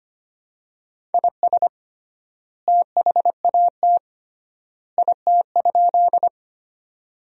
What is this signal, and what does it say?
Morse code, heard as a keyed tone, about 25 wpm, 700 Hz. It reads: IS THAT IT?
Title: 3x Word-Spacing - Sentences from Top 100 Words - 25wpm